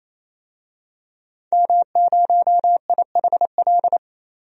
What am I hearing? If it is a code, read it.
Morse code, heard as a keyed tone, about 28 wpm, 700 Hz. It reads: M0IHL
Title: DXCC-100 Call Signs - 5 Characters - Repeated 3x - 28wpm